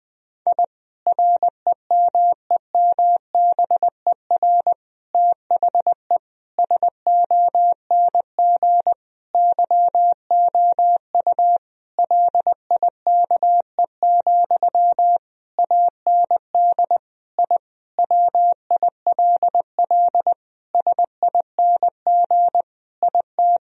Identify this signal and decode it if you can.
Morse code, heard as a keyed tone, about 20 wpm, 700 Hz. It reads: I REMEMBER THE SONG YOU LIKE, AND I WILL SING IT
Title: Sentences from Top 1000 Words - Review Part 2 - 20wpm